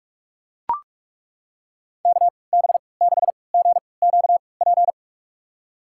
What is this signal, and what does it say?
Morse code, heard as a keyed tone, about 42 wpm, spaced as for 25 wpm, 700 Hz. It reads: KB6GQP
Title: USA Call Signs - 6 Characters - 3x Character Spacing - 45wpm